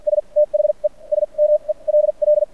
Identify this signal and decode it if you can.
Morse code, heard as a keyed tone, about 50 wpm, spaced as for 38 wpm, 600 Hz. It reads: S THE SMELL
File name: archived / CRSnet